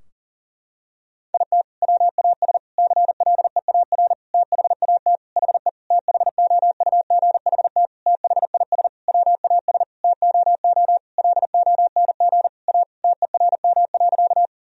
Morse code, heard as a keyed tone, about 40 wpm, 700 Hz. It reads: IT WAS CLEAR THAT HE THOUGHT THIS WAS TOO LONG A TERM.
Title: Sentences from Top 1000 Words - Review Part 4 - 40wpm